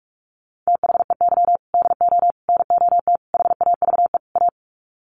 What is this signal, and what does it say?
Morse code, heard as a keyed tone, about 44 wpm, 700 Hz. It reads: THEY DO NOT HAVE A
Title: Sets of 5 Words 45wpm